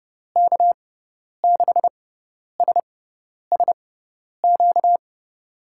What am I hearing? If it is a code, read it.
Morse code, heard as a keyed tone, about 30 wpm, spaced as for 10 wpm, 700 Hz. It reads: K6SSQ
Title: USA Call Signs - 6 Characters - 6x Character Spacing - 30wpm